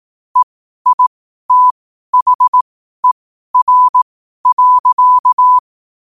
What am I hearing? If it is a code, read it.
Morse code, heard as a keyed tone, about 18 wpm, spaced as for 12 wpm, 1000 Hz. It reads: EITHER.
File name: qotdfiles / 12wpm